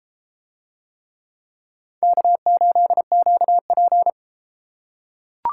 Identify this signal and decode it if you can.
Morse code, heard as a keyed tone, about 30 wpm, 700 Hz. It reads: K8QP